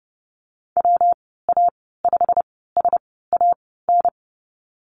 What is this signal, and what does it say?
Morse code, heard as a keyed tone, about 30 wpm, spaced as for 17 wpm, 700 Hz. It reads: WA5SAN
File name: Lesson 19g - Letter w - Call Signs 30wpm